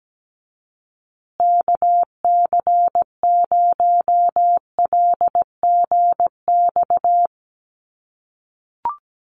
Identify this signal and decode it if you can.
Morse code, heard as a keyed tone, about 17 wpm, 700 Hz. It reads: KC0LGX
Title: USA Call Signs Repeated 2x 17wpm